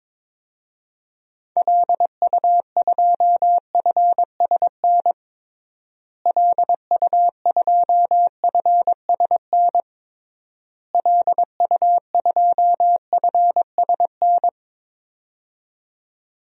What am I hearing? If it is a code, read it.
Morse code, heard as a keyed tone, about 22 wpm, 700 Hz. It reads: LU2FSN LU2FSN LU2FSN
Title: DXCC-100 Call Signs - 6 Characters - Repeated 3x - 22wpm